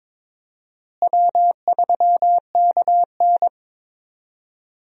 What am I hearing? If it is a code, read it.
Morse code, heard as a keyed tone, about 21 wpm, 700 Hz. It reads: W3KN
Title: USA Call Signs - 4 Characters - Repeated 2x - 22wpm